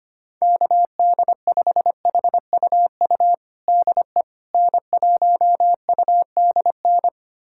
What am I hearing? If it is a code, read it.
Morse code, heard as a keyed tone, about 25 wpm, 700 Hz. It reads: KD5HUU DE N1UDN
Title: USA Callsign DE Callsign 25wpm